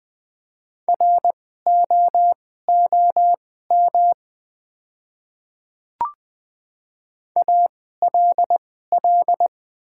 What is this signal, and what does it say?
Morse code, heard as a keyed tone, about 20 wpm, spaced as for 14 wpm, 700 Hz. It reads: ROOM ALL
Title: 2x Character Spacing - Top 300 Words - 20wpm